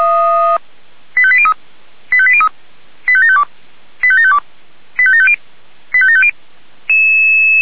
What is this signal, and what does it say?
Five-tone selective call (ZVEI), heard as quick runs of five tones, tones of about 70 ms. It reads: 85803 85803 85752 85752 85750 85750
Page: /464/SDR/ZVEI/85827/